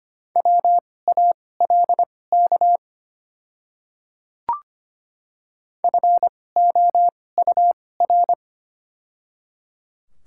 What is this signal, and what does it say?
Morse code, heard as a keyed tone, about 25 wpm, spaced as for 18 wpm, 700 Hz. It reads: WALK FOUR
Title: 2x Character Spacing - Top 300 Words - 25wpm